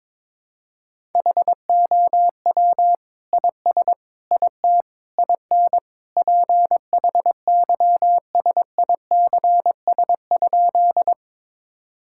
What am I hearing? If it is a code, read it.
Morse code, heard as a keyed tone, about 22 wpm, 700 Hz. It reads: HOW IS IT IN PHYSICS?
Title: Sentences from Top 1000 Words - Review Part 5 - 22wpm